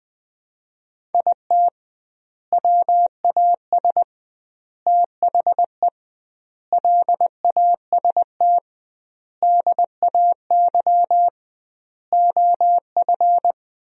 Morse code, heard as a keyed tone, about 20 wpm, 700 Hz. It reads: IT WAS THE LAST DAY OF
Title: 2x Word-Spacing - Sentences from Top 300 Words - 20wpm